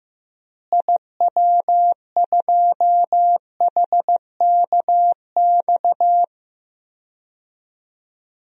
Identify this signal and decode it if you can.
Morse code, heard as a keyed tone, about 15 wpm, 700 Hz. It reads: IW2HKX
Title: DXCC-100 Call Signs - 6 Characters - Repeated 3x - 15wpm